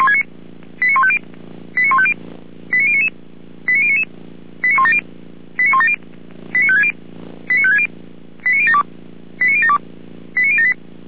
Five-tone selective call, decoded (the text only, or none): 89150 89150 89099 89099 89170 89170 89560 89560 89072 89072 89078